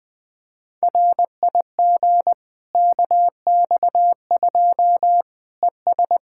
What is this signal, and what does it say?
Morse code, heard as a keyed tone, about 21 wpm, 700 Hz. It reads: RIG KX2 ES
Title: Practice QSO - Exchange 2 - Info - 20wpm